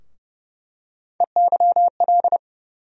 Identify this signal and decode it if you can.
Morse code, heard as a keyed tone, about 30 wpm, 700 Hz. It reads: EYL
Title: USA Call Sign Postfixes - 3 Letters 30wpm